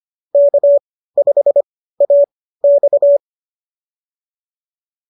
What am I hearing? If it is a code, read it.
Morse code, heard as a keyed tone, about 25 wpm, spaced as for 15 wpm, 590 Hz. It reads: K5AX
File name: qso201 15